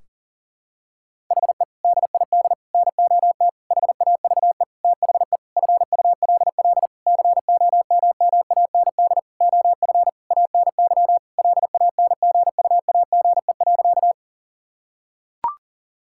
Morse code, heard as a keyed tone, about 40 wpm, 700 Hz. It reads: HE DID NOT HAVE THE FULL COMMAND OF ANY LANGUAGE.
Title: Sentences from Top 1600 Words 40wpm